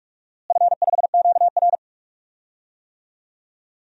Morse code, heard as a keyed tone, about 44 wpm, 700 Hz. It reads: F5QR
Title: DXCC-100 Call Signs - 4 Characters - Repeated 2x - 45wpm